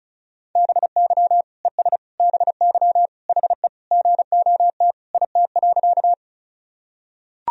Morse code, heard as a keyed tone, about 35 wpm, 700 Hz. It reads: BY ES BY HE GOT IT.
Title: Sentences from Top 300 Words and Abbreviations 35wpm